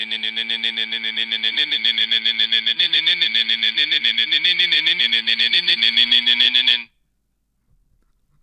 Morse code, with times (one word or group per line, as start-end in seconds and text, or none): none